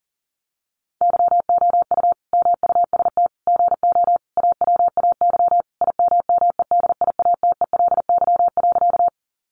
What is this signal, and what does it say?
Morse code, heard as a keyed tone, about 40 wpm, 700 Hz. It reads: YOU MUST GO AWAY IMMEDIATELY.
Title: Sentences from Top 1100 Words 40wpm